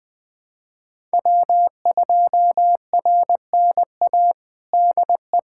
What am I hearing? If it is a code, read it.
Morse code, heard as a keyed tone, about 20 wpm, 700 Hz. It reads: W2RNA DE
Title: USA Callsign DE Callsign - 5 Characters - 20wpm